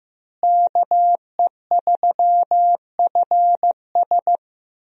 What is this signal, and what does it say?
Morse code, heard as a keyed tone, about 15 wpm, 700 Hz. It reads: KE3FS
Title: USA Call Signs - 5 Characters - Repeated 2x - 15wpm